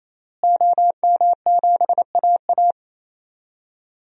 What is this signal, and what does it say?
Morse code, heard as a keyed tone, about 27 wpm, 700 Hz. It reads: OM7AA
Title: DXCC-100 Call Signs - 5 Characters - Repeated 3x - 28wpm